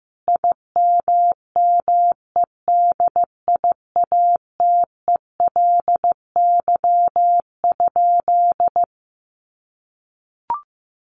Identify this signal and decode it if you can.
Morse code, heard as a keyed tone, about 15 wpm, 700 Hz. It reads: IMMEDIATELY?
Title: Sentences from Top 1300 Words 15wpm